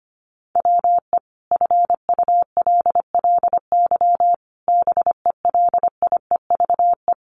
Morse code, heard as a keyed tone, about 25 wpm, 700 Hz. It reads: WE FULLY BELIEVE
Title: Sentences from Top 1300 Words 25wpm